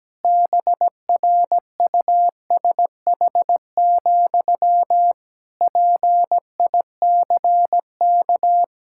Morse code, heard as a keyed tone, about 17 wpm, 700 Hz. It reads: BRUSH, PICK
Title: Pangram Challenge v2 - 17wpm